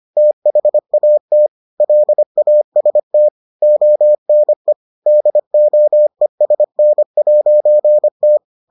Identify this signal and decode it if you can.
Morse code, heard as a keyed tone, about 25 wpm, 600 Hz. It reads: THAT LAST ONE DOESN'T